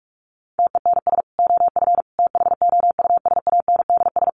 Morse code, heard as a keyed tone, about 44 wpm, 700 Hz. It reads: TENS OF THOUSANDS
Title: Sets of 4 Words 45wpm